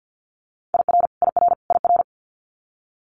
Morse code, heard as a keyed tone, about 50 wpm, 700 Hz. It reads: IR IR IR
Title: Multi-letter Phonemes Repeated 3x 50wpm